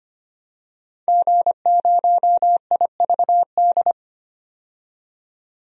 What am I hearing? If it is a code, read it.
Morse code, heard as a keyed tone, about 25 wpm, 700 Hz. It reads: G0IVD
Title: DXCC-100 Call Signs - 5 Characters - 25wpm